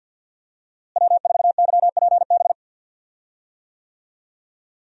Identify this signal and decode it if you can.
Morse code, heard as a keyed tone, about 52 wpm, 700 Hz. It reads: W4YPB
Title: USA Call Signs - 5 Characters - Repeated 2x - 50wpm